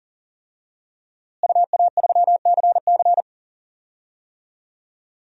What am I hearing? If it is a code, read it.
Morse code, heard as a keyed tone, about 37 wpm, 700 Hz. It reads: UA3CC
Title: DXCC-100 Call Signs - 5 Characters - Repeated 2x - 40wpm